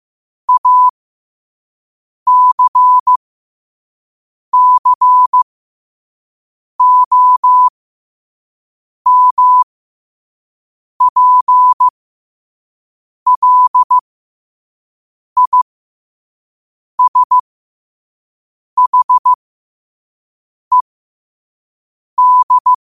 Morse code, heard as a keyed tone, about 15 wpm, spaced as for 5 wpm, 1000 Hz. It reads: ACCOMPLISHED
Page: Quotes for Fri, 15 Aug 2025 in Morse Code at 5 words per minute.